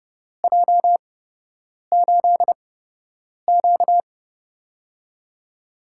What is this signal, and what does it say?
Morse code, heard as a keyed tone, about 30 wpm, spaced as for 8 wpm, 700 Hz. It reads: J8Q
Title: USA Call Signs - 4 Characters - 8x Character Spacing - 30wpm